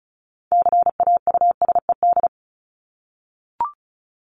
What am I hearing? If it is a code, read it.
Morse code, heard as a keyed tone, about 34 wpm, 700 Hz. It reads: CAUSED
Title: ED Suffix - Top 100 Words with Base Words 35wpm